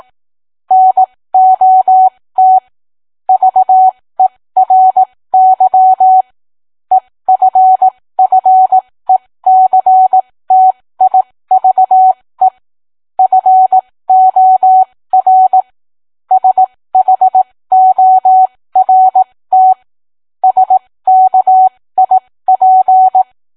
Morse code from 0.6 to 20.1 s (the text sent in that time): NOT VERY EFFECTIVE FOR SHORT